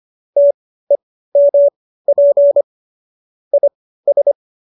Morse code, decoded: TEMP IS

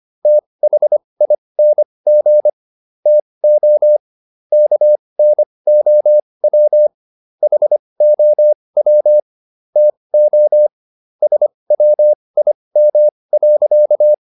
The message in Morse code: THING TO KNOW HOW TO SWIM.

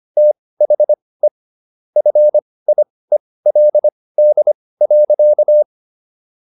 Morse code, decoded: THE FIELD.

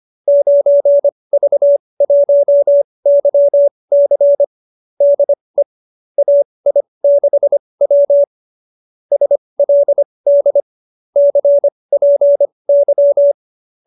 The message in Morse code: 9V1YC DE AI6W SLD CPY

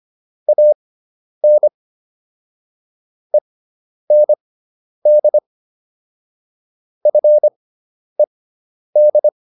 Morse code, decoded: AN END FED

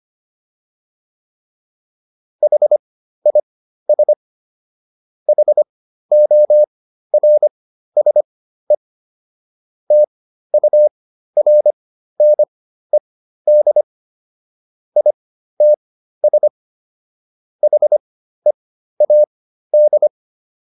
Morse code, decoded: HIS HORSE TURNED ITS HEAD

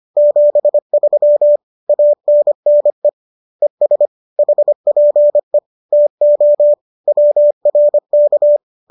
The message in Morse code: 73 ANNE ES HPE TO WRK

